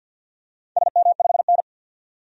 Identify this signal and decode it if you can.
Morse code, heard as a keyed tone, about 46 wpm, 700 Hz. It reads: SM5N
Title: DXCC-100 Call Signs - 4 Characters - 50wpm